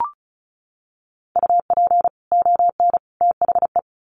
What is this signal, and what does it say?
Morse code, heard as a keyed tone, about 34 wpm, 700 Hz. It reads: UP ON THE
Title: Sets of 3 Words 35wpm